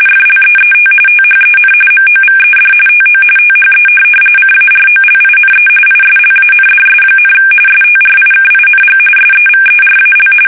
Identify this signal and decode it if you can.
RTTY, about 45.45 baud. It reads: FLEET THEN MOVED TO THE CAROLINE ISLANDS, STRIKING SEVERAL
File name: fsk850 / USS-Massachusetts